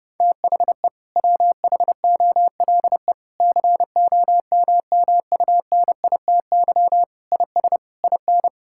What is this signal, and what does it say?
Morse code, heard as a keyed tone, about 32 wpm, 700 Hz. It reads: THE WHOLE COMMUNITY IS IN
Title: Sentences from Top 1600 Words 30wpm